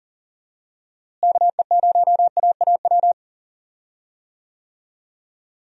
Morse code, heard as a keyed tone, about 40 wpm, 700 Hz. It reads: KE0AAW
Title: USA Call Signs - 6 Characters - Repeated 2x - 40wpm